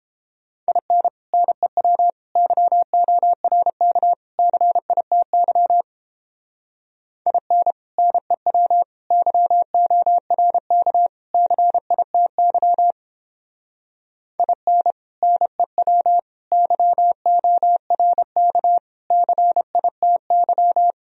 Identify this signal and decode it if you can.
Morse code, heard as a keyed tone, about 29 wpm, 700 Hz. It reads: IN NEW YORK CITY IN NEW YORK CITY IN NEW YORK CITY